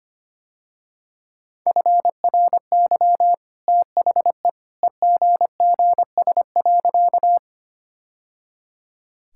Morse code, heard as a keyed tone, about 25 wpm, 700 Hz. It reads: FRY THE EGGS.
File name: Easy Sentences with Hints - 25wpm